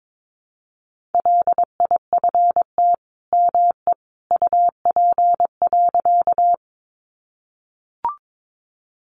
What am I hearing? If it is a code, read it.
Morse code, heard as a keyed tone, about 22 wpm, 700 Hz. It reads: LIFT ME UP.
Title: Sentences from Top 1000 Words 22wpm